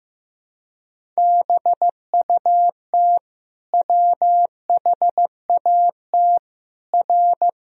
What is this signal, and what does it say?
Morse code, heard as a keyed tone, about 15 wpm, 700 Hz. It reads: BUT WHAT R
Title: Sentences from Top 100 Words and Abbreviations 15wpm